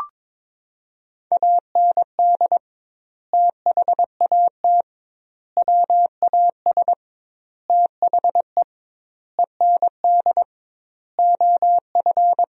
Morse code, heard as a keyed tone, about 21 wpm, 700 Hz. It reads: AND THAT WAS THE END OF
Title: 2x Word-Spacing - Sentences from Top 200 Words - 22wpm